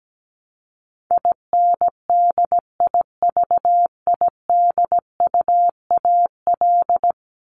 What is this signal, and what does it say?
Morse code, heard as a keyed tone, about 17 wpm, 700 Hz. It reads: INDIVIDUAL
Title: ES:S Suffix - Top 100 Words with Base Words 17wpm